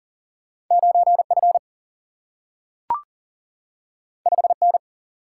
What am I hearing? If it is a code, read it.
Morse code, heard as a keyed tone, about 39 wpm, 700 Hz. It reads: 9F 5N
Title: Single-Letter Single-Number 40wpm